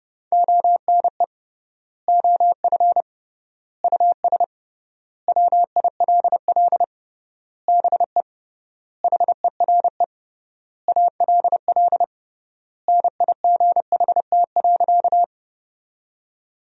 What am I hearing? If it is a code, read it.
Morse code, heard as a keyed tone, about 30 wpm, 700 Hz. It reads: ONE OF US WILL BE HERE ALL NIGHT.
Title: 3x Word-Spacing - Sentences from Top 300 Words - 30wpm